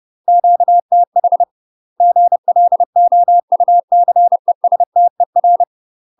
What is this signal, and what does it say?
Morse code, heard as a keyed tone, about 30 wpm, 700 Hz. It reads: QTH GLOUCESTER
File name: UK MASTER QSOs TRAINING / 30WPM / UK MASTER QSOs 0034